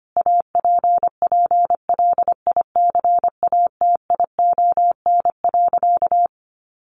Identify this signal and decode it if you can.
Morse code, heard as a keyed tone, about 25 wpm, 700 Hz. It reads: APPLICATION.
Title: Sentences from Top 1200 Words 25wpm